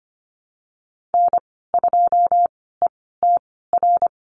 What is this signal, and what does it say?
Morse code, heard as a keyed tone, about 25 wpm, spaced as for 16 wpm, 700 Hz. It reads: N2ETR
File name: Lesson 15g - Number 5 - Call Signs 25wpm